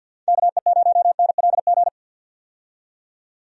Morse code, heard as a keyed tone, about 53 wpm, 700 Hz. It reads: KE0NLG
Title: USA Call Signs - 6 Characters - 50wpm